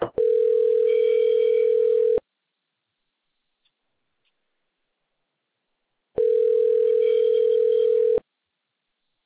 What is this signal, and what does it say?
Ringback tone (North America), 440 + 480 Hz.